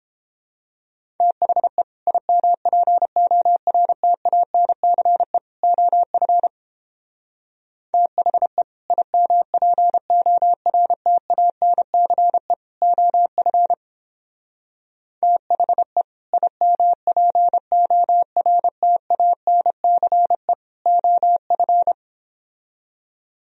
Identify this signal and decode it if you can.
Morse code, heard as a keyed tone, about 31 wpm, 700 Hz. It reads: THE IMPORTANCE OF THE IMPORTANCE OF THE IMPORTANCE OF